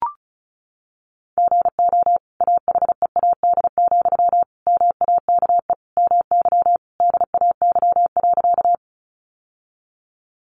Morse code, heard as a keyed tone, about 34 wpm, 700 Hz. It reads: GO AHEAD, MAKE MY DAY.
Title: Sentences from Top 1100 Words 35wpm